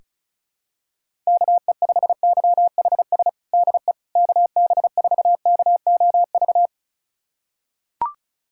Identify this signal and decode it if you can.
Morse code, heard as a keyed tone, about 36 wpm, 700 Hz. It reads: KE5YHS DE KB4KOV
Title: USA Callsign DE Callsign - 6 Characters - 35wpm